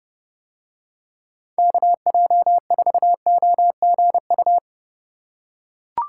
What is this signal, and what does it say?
Morse code, heard as a keyed tone, about 32 wpm, 700 Hz. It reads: KJ4OGU